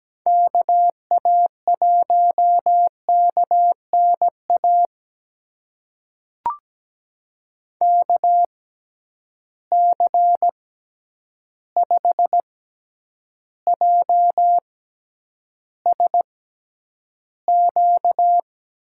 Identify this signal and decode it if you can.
Morse code, heard as a keyed tone, about 17 wpm, spaced as for 9 wpm, 700 Hz. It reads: KA1KNA KC5JSQ
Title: USA Call Signs - 6 Characters - 6x Character Spacing - 17wpm